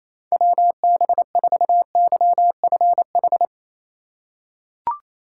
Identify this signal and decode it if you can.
Morse code, heard as a keyed tone, about 28 wpm, 700 Hz. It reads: WB4YFH
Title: USA Call Signs - 6 Characters - 4x Character Spacing - 28wpm